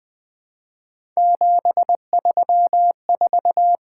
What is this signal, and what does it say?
Morse code, heard as a keyed tone, about 20 wpm, 700 Hz. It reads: 734
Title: Three Letter-Number Combinations - 20wpm